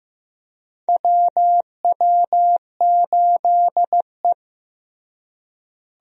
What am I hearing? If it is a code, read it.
Morse code, heard as a keyed tone, about 16 wpm, 700 Hz. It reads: WW8E
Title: USA Call Signs - 4 Characters - Repeated 3x - 15wpm